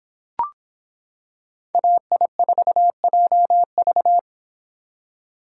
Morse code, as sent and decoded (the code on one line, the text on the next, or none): .- .. ....- .--- ...-
AI4JV